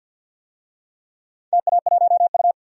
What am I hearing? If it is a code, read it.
Morse code, heard as a keyed tone, about 52 wpm, 700 Hz. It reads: TA1U
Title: DXCC-100 Call Signs - 4 Characters - 50wpm